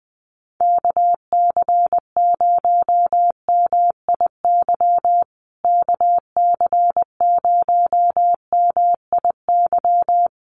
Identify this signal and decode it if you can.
Morse code, heard as a keyed tone, about 20 wpm, 700 Hz. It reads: KC0MIY KC0MIY
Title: USA Call Signs Repeated 3x 20wpm